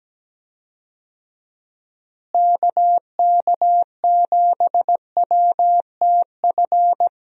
Morse code, heard as a keyed tone, about 17 wpm, 700 Hz. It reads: KK7WTF
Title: USA Call Signs - 6 Characters - Repeated 2x - 17wpm